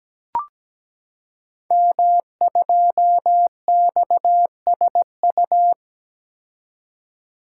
Morse code, heard as a keyed tone, about 17 wpm, 700 Hz. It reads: M2XSU
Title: Code Groups - 5 Letter:Numbers - 17wpm